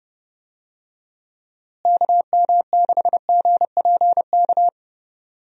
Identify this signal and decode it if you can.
Morse code, heard as a keyed tone, about 30 wpm, 700 Hz. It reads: KM6GPK
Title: USA Call Signs - 6 Characters - Repeated 2x - 30wpm